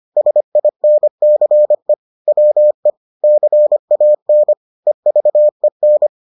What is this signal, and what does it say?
Morse code, heard as a keyed tone, about 25 wpm, 600 Hz. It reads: SINCE WE CAN EVEN